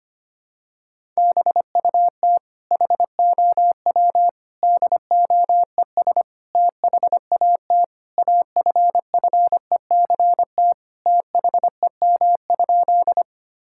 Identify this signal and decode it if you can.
Morse code, heard as a keyed tone, about 25 wpm, 700 Hz. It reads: BUT HOW DOES THAT AFFECT THEM?
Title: Sentences from Top 1000 Words - Review Part 6 - 25wpm